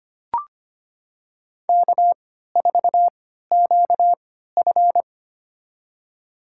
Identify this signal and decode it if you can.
Morse code, heard as a keyed tone, about 25 wpm, spaced as for 14 wpm, 700 Hz. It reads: K4QF
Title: USA Call Signs - 4 Characters - 3x Character Spacing - 25wpm